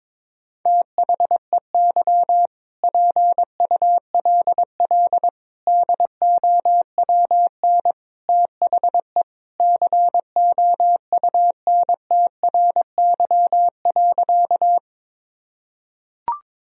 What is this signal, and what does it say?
Morse code, heard as a keyed tone, about 21 wpm, 700 Hz. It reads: THEY PULL DOWN THE COUNTRY.
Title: Sentences from Top 1000 Words - Review Part 2 - 22wpm